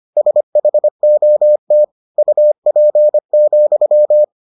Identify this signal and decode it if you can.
Morse code, heard as a keyed tone, about 25 wpm, 600 Hz. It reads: SHOT UP,